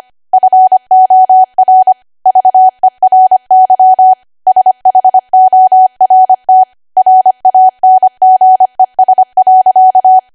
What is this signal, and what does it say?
Morse code, heard as a keyed tone, about 25 wpm, 750 Hz. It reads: FOR VERY SHORT RANGES.
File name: Archive / 25 WPM / 25WPM